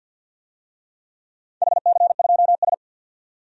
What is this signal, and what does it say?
Morse code, heard as a keyed tone, about 53 wpm, 700 Hz. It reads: HC2S
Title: DXCC-100 Call Signs - 4 Characters - 50wpm